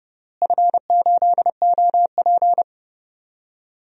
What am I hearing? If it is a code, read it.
Morse code, heard as a keyed tone, about 30 wpm, 700 Hz. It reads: F8OP
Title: DXCC-100 Call Signs - 4 Characters - Repeated 3x - 30wpm